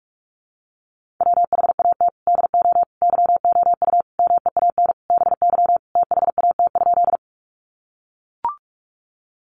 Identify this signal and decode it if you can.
Morse code, heard as a keyed tone, about 44 wpm, 700 Hz. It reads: WHAT DO YOU MEAN BY THAT?
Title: Sentences from Top 200 Words 45wpm